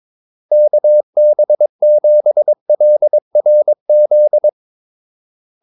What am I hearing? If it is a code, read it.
Morse code, heard as a keyed tone, about 22 wpm, 600 Hz. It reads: KB7LRZ
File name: Callsigns-22wpm